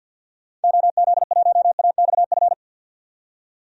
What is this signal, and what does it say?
Morse code, heard as a keyed tone, about 46 wpm, 700 Hz. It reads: OZ1AXF
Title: DXCC-100 Call Signs - 6 Characters - 50wpm